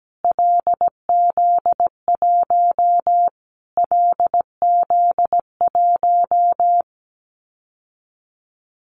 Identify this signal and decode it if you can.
Morse code, heard as a keyed tone, about 17 wpm, 700 Hz. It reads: LZ1 LZ1
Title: Callsign Prefixes for Top 25 DXCC - Repeated 3x - 17wpm